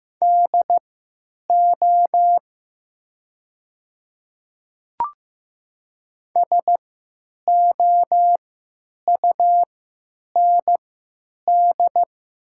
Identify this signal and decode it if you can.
Morse code, heard as a keyed tone, about 15 wpm, spaced as for 9 wpm, 700 Hz. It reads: DO SOUND
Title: 3x Character Spacing - Top 100 Words - 15wpm